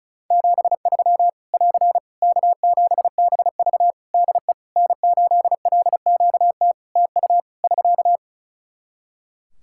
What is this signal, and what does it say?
Morse code, heard as a keyed tone, about 36 wpm, 700 Hz. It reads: 73 <AR> K7BV DE N8LQT TU <SK>
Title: Practice QSO - Exchange 4 - Ending - 35wpm